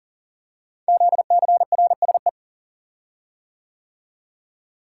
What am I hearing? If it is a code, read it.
Morse code, heard as a keyed tone, about 38 wpm, 690 Hz. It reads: ZCRSE
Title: Code Groups - 5 Letters - 40wpm